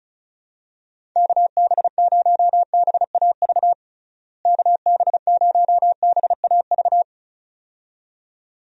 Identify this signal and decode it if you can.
Morse code, heard as a keyed tone, about 36 wpm, 700 Hz. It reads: KB0BAV KB0BAV
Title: USA Call Signs - 6 Characters - Repeated 2x - 35wpm